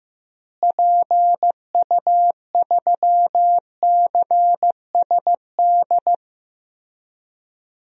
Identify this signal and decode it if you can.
Morse code, heard as a keyed tone, about 16 wpm, 700 Hz. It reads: PU3CSD
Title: DXCC-100 Call Signs - 6 Characters - Repeated 2x - 15wpm